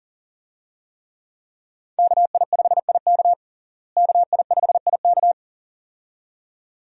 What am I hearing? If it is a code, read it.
Morse code, heard as a keyed tone, about 37 wpm, 700 Hz. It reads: KI5IK KI5IK
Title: USA Call Signs - 5 Characters - Repeated 2x - 40wpm